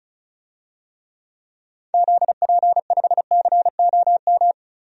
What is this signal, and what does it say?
Morse code, heard as a keyed tone, about 36 wpm, 700 Hz. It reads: ZP5COM
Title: DXCC-100 Call Signs - 6 Characters - 35wpm